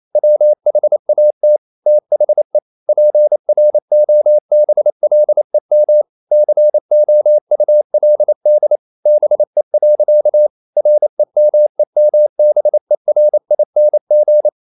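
Morse code, heard as a keyed tone, about 28 wpm, 600 Hz. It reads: WHAT THE PROBLEM COULD BE. REMEMBERING
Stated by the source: Morse code practice at 28 wpm, text of Ken Follett's novel Never in English, from CW med Gnister.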